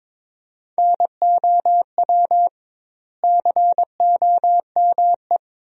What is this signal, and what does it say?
Morse code, heard as a keyed tone, about 22 wpm, 700 Hz. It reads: NOW COME
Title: 2x Word-Spacing - Sentences from Top 200 Words - 22wpm